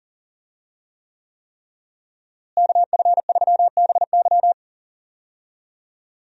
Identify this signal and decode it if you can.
Morse code, heard as a keyed tone, about 38 wpm, 700 Hz. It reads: KF3BY
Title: USA Call Signs - 5 Characters - Repeated 3x - 40wpm